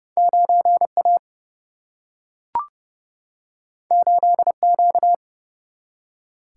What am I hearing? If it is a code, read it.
Morse code, heard as a keyed tone, about 30 wpm, 700 Hz. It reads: 9A 8Q
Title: Single-Letter Single-Number 30wpm